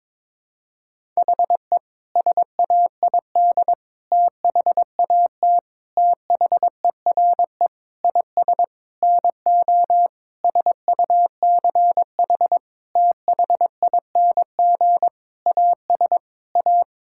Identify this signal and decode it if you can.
Morse code, heard as a keyed tone, about 22 wpm, 700 Hz. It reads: HE SAID THAT THERE IS NO SUCH THING AS A
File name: practice / 22wpm / Sentences with 8-Word Phrases - 22wpm